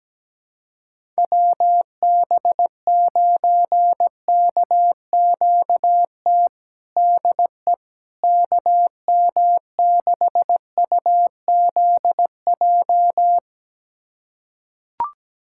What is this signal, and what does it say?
Morse code, heard as a keyed tone, about 17 wpm, 700 Hz. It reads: WB9KQT DE KM6UZJ
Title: USA Callsign DE Callsign - 6 Characters - 17wpm